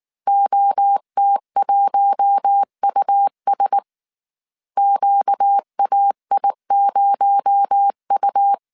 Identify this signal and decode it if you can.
Morse code, heard as a keyed tone, about 19 wpm, 800 Hz. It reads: OT1US QAI0U
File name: CW / DET / 1 6 1500C S A 18wpm 2255